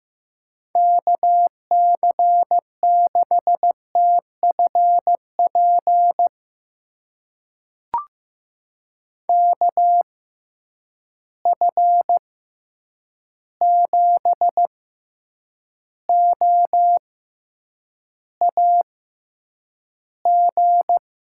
Morse code, heard as a keyed tone, about 15 wpm, spaced as for 8 wpm, 700 Hz. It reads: KC6TFP KF7OAG
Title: USA Call Signs - 6 Characters - 6x Character Spacing - 15wpm